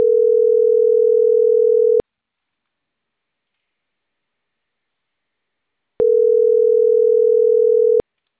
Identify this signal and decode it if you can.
Ringback tone (North America), 440 + 480 Hz.
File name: CENTRAL CAEA / out agcaea01 - .79666